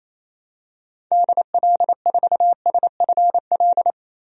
Morse code, heard as a keyed tone, about 28 wpm, 700 Hz. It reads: DL4SFL
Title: DXCC-100 Call Signs - 6 Characters - 28wpm